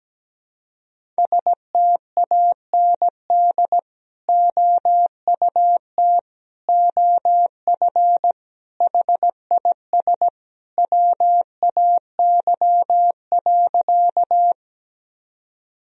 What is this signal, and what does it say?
Morse code, heard as a keyed tone, about 17 wpm, 700 Hz. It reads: STAND OUT OF HIS WAY.
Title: Sentences from Top 1000 Words - Review Part 1 - 17wpm